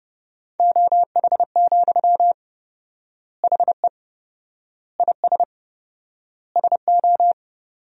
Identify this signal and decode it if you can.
Morse code, heard as a keyed tone, about 30 wpm, 700 Hz. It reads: OH, HE IS SO
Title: 4x Word-Spacing - Sentences from Top 500 Words - 30wpm